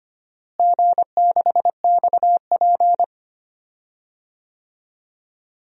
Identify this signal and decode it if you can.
Morse code, heard as a keyed tone, about 25 wpm, 700 Hz. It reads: G6XP
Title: DXCC-100 Call Signs - 4 Characters - Repeated 3x - 25wpm